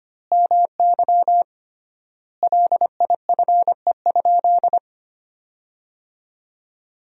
Morse code, heard as a keyed tone, about 25 wpm, 700 Hz. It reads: MY LIFE?
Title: 3x Word-Spacing - Sentences from Top 300 Words - 25wpm